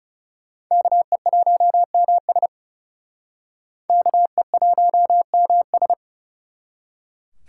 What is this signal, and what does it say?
Morse code, heard as a keyed tone, about 39 wpm, 700 Hz. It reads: KE1MS KE1MS